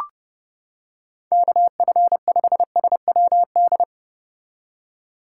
Morse code, heard as a keyed tone, about 29 wpm, 700 Hz. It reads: KF5SWD